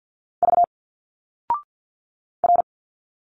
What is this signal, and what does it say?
Morse code, heard as a keyed tone, about 50 wpm, 700 Hz. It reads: V R
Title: Single Letters 50wpm